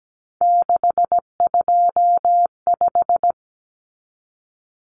Morse code, heard as a keyed tone, about 17 wpm, 700 Hz. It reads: 625
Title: Numbers - 3-Digit 17wpm